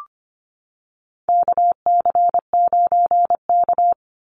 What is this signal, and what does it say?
Morse code, heard as a keyed tone, about 25 wpm, 700 Hz. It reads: KC9K
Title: Call Signs - 25wpm